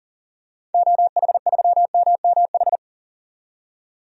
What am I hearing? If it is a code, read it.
Morse code, heard as a keyed tone, about 39 wpm, 700 Hz. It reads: OH3MMH
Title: DXCC-100 Call Signs - 6 Characters - 40wpm